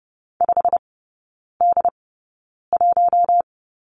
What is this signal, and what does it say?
Morse code, heard as a keyed tone, about 30 wpm, spaced as for 9 wpm, 700 Hz. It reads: 5D1
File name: Lesson 15e - Number 5 - Three Characters 30wpm